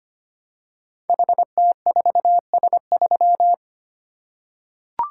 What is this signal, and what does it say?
Morse code, heard as a keyed tone, about 25 wpm, 700 Hz. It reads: HT4S3
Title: Code Groups - 5 Letter:Numbers - 25wpm